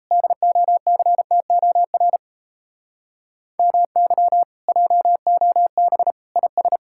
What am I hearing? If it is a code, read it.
Morse code, heard as a keyed tone, about 34 wpm, 700 Hz. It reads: DOCTOR MY JOB IS
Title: Speed-Racing Common QSO Phrases - 28wpm